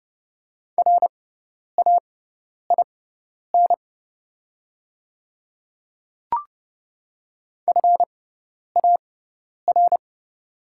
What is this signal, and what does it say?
Morse code, heard as a keyed tone, about 30 wpm, spaced as for 10 wpm, 700 Hz. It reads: RAIN FAR
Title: 6x Character Spacing - Top 500 Words - 30wpm